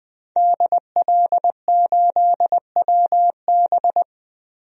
Morse code, heard as a keyed tone, about 21 wpm, 700 Hz. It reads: DL8WB
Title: DXCC-100 Call Signs - 5 Characters - Repeated 2x - 20wpm